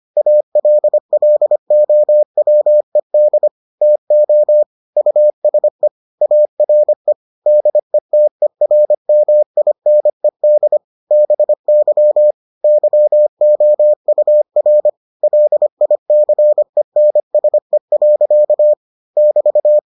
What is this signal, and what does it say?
Morse code, heard as a keyed tone, about 25 wpm, 600 Hz. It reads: ALLOWED TO USE ARE DETERMINED BY YOUR LICENSE. <BT>